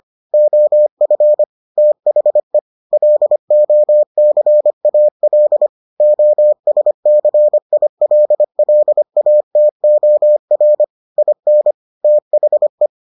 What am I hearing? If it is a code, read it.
Morse code, heard as a keyed tone, about 25 wpm, 600 Hz. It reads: OF THE LOCAL OSCILLATOR IN THE